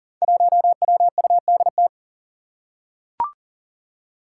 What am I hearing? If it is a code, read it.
Morse code, heard as a keyed tone, about 40 wpm, 700 Hz. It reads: 1WUDT
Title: Five Challenging Letter-Number Combinations 40wpm